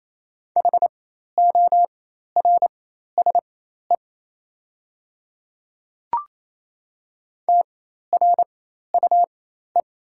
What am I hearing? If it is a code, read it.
Morse code, heard as a keyed tone, about 27 wpm, spaced as for 13 wpm, 690 Hz. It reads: HORSE TRUE
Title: 4x Character Spacing - Top 500 Words - 28wpm